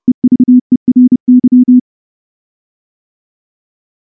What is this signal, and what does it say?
Morse code, heard as a keyed tone, about 28 wpm, 250 Hz. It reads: EVERY
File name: morse / morse-training-30WPM